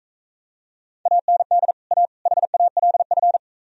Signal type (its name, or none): Morse code